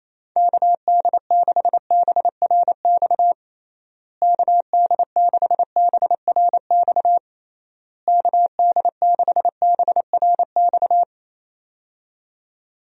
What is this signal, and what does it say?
Morse code, heard as a keyed tone, about 28 wpm, 700 Hz. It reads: KD6BRX KD6BRX KD6BRX
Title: USA Call Signs - 6 Characters - Repeated 3x - 28wpm